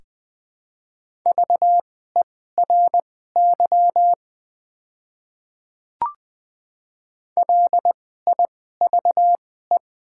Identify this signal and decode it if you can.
Morse code, heard as a keyed tone, about 20 wpm, spaced as for 15 wpm, 700 Hz. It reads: VERY LIVE
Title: 2x Character Spacing - Top 200 Words - 20wpm